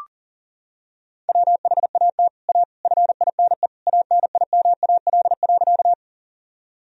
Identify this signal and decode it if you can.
Morse code, heard as a keyed tone, about 40 wpm, 700 Hz. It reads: WHAT A FINE ANIMAL.
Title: Sentences from Top 1000 Words - Review Part 1 - 40wpm